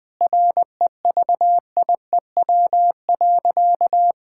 Morse code, heard as a keyed tone, about 21 wpm, 700 Hz. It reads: REVIEW.
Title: Sentences from Top 1000 Words - Review Part 6 - 20wpm